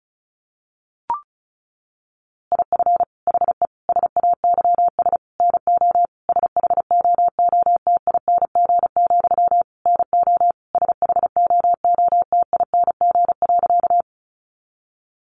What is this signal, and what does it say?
Morse code, heard as a keyed tone, about 34 wpm, 700 Hz. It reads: IF HE SAYS NO SHOOTING, NO SHOOTING.
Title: A Sound Of Thunder 35wpm